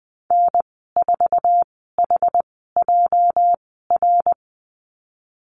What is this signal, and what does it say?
Morse code, heard as a keyed tone, about 20 wpm, spaced as for 14 wpm, 700 Hz. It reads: N4HJR
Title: 245 Lesson 39g - Letter z - Call Signs 20wpm